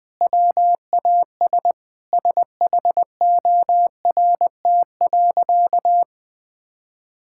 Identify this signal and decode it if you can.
Morse code, heard as a keyed tone, about 20 wpm, 700 Hz. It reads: WAS SHORT.
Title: Sentences from Top 1600 Words 20wpm